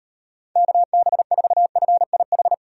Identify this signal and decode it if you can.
Morse code, heard as a keyed tone, about 38 wpm, 700 Hz. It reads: KB4FIH